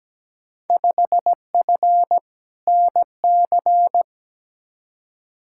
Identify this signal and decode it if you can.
Morse code, heard as a keyed tone, about 17 wpm, 700 Hz. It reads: 5F NC
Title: Field Day Exchanges Encore - 17wpm